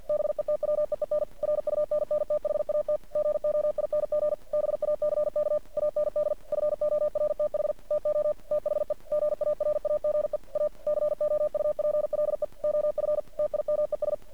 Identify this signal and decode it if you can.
Morse code, heard as a keyed tone, about 50 wpm, 600 Hz. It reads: BETWEEN RUNNTHAT GOING BACK AND FORTH TO THE GARAGE A COUPLE OF TIMES
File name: archived / CRSnet